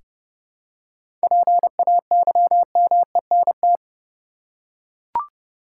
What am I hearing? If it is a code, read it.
Morse code, heard as a keyed tone, about 30 wpm, 700 Hz. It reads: PAYMENT
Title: Seven Letter Words - 30wpm